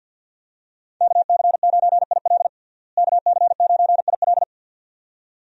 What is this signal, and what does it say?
Morse code, heard as a keyed tone, about 49 wpm, 700 Hz. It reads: KC9IL KC9IL
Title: USA Call Signs - 5 Characters - Repeated 2x - 50wpm